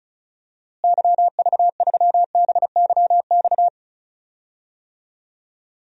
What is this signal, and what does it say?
Morse code, heard as a keyed tone, about 36 wpm, 700 Hz. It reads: YV3BYX
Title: DXCC-100 Call Signs - 6 Characters - Repeated 2x - 35wpm